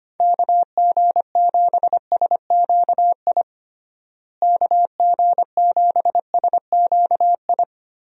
Morse code, heard as a keyed tone, about 25 wpm, 700 Hz. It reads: KG7SQI KG7SQI
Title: USA Call Signs - 6 Characters - Repeated 3x - 25wpm